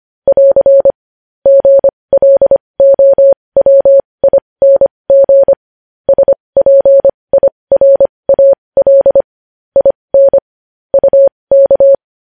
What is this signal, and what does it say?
Morse code, heard as a keyed tone, about 25 wpm, spaced as for 20 wpm, 550 Hz. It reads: <AR> GLOWING SPIRAL IN UK
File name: podcast / News headlines-1x-25x20